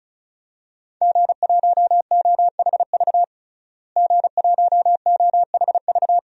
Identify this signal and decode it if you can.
Morse code, heard as a keyed tone, about 34 wpm, 700 Hz. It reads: G1OHV G1OHV
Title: DXCC-100 Call Signs - 5 Characters - Repeated 2x - 35wpm